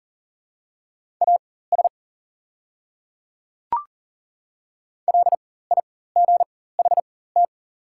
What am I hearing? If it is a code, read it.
Morse code, heard as a keyed tone, about 38 wpm, spaced as for 18 wpm, 690 Hz. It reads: AS LIGHT
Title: 4x Character Spacing - Top 500 Words - 40wpm